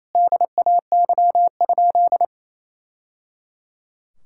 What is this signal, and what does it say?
Morse code, heard as a keyed tone, about 28 wpm, 700 Hz. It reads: DAY?
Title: Sentences from Top 1000 Words - Review Part 5 - 28wpm